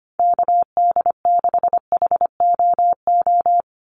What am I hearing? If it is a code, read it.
Morse code, heard as a keyed tone, about 25 wpm, 700 Hz. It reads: KD6HOO
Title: USA Call Signs 25wpm